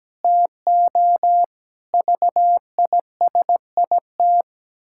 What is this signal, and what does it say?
Morse code, heard as a keyed tone, about 17 wpm, 700 Hz. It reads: TO VISIT
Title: Pangram Challenge v2 - 17wpm